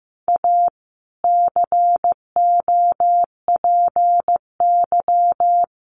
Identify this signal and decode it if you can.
Morse code, heard as a keyed tone, about 15 wpm, 700 Hz. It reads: A COPY
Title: Sentences from Top 1400 Words 15wpm